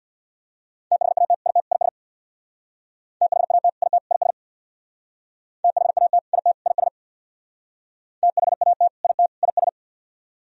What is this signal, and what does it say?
Morse code, heard as a keyed tone, about 62 wpm, 700 Hz. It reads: THAT IT IS THAT IT IS THAT IT IS THAT IT IS
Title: Speed-Racing - Sets of 3 Words 50wpm